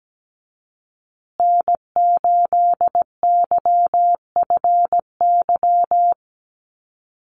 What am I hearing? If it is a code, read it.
Morse code, heard as a keyed tone, about 17 wpm, 700 Hz. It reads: N8YFY
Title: USA Call Signs Repeated 2x 17wpm